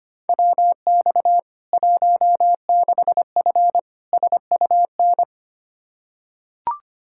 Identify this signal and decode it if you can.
Morse code, heard as a keyed tone, about 25 wpm, 700 Hz. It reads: WX 16F SUN
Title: K1USN SST QSO Practice 25wpm